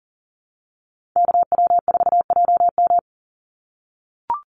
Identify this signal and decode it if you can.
Morse code, heard as a keyed tone, about 40 wpm, 700 Hz. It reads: KW4JM
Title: NA SOTA Activator Callsigns 40wpm